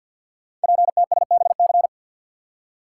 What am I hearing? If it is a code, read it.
Morse code, heard as a keyed tone, about 48 wpm, 700 Hz. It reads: PTSBC
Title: Code Groups - 5 Letters - 50wpm